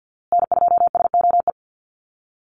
Morse code, heard as a keyed tone, about 50 wpm, 700 Hz. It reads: N2SOE
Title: USA Call Signs Repeated 2x 50wpm